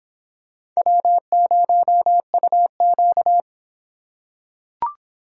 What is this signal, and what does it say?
Morse code, heard as a keyed tone, about 24 wpm, 700 Hz. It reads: W0UQ